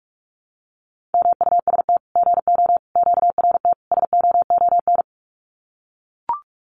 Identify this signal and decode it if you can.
Morse code, heard as a keyed tone, about 44 wpm, 700 Hz. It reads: MUST GO QRT SOON
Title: Common QSO Phrases 45wpm